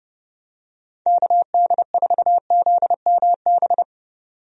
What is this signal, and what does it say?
Morse code, heard as a keyed tone, about 30 wpm, 700 Hz. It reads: KD4ZMB
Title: USA Call Signs - 6 Characters - 30wpm